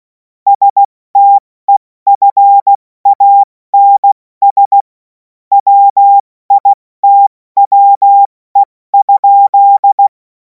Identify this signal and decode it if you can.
Morse code, heard as a keyed tone, about 16 wpm, 800 Hz. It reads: STEFANS WITWE?
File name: Morsekurs / Witze / Witz 07. April 2026 70 F80